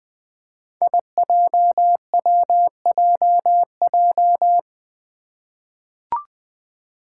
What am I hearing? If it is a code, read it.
Morse code, heard as a keyed tone, about 21 wpm, 700 Hz. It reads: IJWJJ
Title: Five Challenging Letter-Number Combinations 20wpm